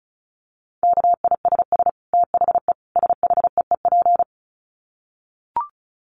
Morse code, heard as a keyed tone, about 34 wpm, 700 Hz. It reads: KISS THE SHEEP
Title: Dit Challenge 35wpm